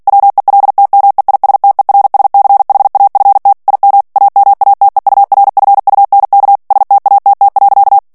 Morse code, heard as a keyed tone, about 48 wpm, 800 Hz. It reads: WELTMEISTERSCHAFT IM AMATEURFUNK STATT.
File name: morse mp3 / ALLEMAND texte clair 2 vitesse 45 mots minute